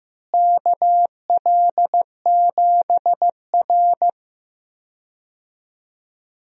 Morse code, heard as a keyed tone, about 15 wpm, 700 Hz. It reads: KL7R
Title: DXCC-100 Call Signs - 4 Characters - Repeated 3x - 15wpm